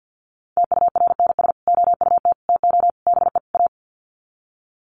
Morse code, heard as a keyed tone, about 50 wpm, 700 Hz. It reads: TURNS OUT TO BE A
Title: Sets of 5 Words Encore 50wpm